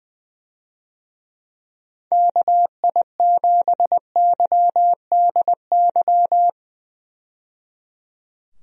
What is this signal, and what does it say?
Morse code, heard as a keyed tone, about 20 wpm, 700 Hz. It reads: KI7YDY
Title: USA Call Signs - 6 Characters - 20wpm